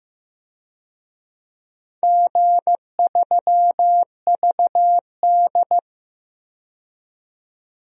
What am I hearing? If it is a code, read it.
Morse code, heard as a keyed tone, about 15 wpm, 700 Hz. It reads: G3VD
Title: DXCC-100 Call Signs - 4 Characters - 15wpm